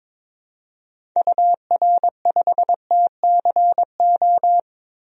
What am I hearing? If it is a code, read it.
Morse code, heard as a keyed tone, about 21 wpm, 700 Hz. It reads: UR5TCO
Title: DXCC-100 Call Signs - 6 Characters - Repeated 2x - 22wpm